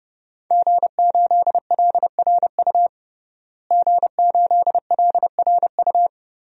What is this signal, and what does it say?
Morse code, heard as a keyed tone, about 30 wpm, 700 Hz. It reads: G8LRU G8LRU
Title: DXCC-100 Call Signs - 5 Characters - Repeated 3x - 30wpm